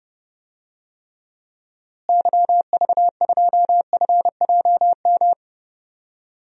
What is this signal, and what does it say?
Morse code, heard as a keyed tone, about 30 wpm, 700 Hz. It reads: YV2FJM
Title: DXCC-100 Call Signs - 6 Characters - Repeated 2x - 30wpm